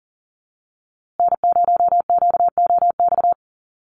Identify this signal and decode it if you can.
Morse code, heard as a keyed tone, about 40 wpm, 700 Hz. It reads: N0QOX
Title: USA Call Signs Repeated 3x 40wpm